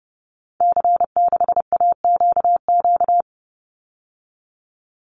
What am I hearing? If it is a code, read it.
Morse code, heard as a keyed tone, about 30 wpm, 700 Hz. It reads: C6AQQ
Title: Call Signs Repeated 2x - 30wpm